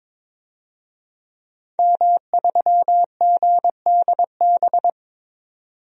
Morse code, heard as a keyed tone, about 22 wpm, 700 Hz. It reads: M3GDB
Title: DXCC-100 Call Signs - 5 Characters - 22wpm